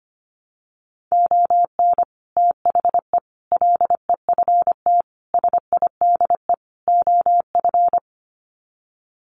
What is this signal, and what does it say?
Morse code, heard as a keyed tone, about 25 wpm, 700 Hz. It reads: ON THE LEFT SIDE OF
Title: Sets of 5 Words Encore2 25wpm